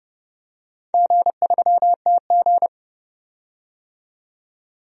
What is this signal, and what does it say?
Morse code, heard as a keyed tone, about 31 wpm, 700 Hz. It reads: G3TG
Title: DXCC-100 Call Signs - 4 Characters - 30wpm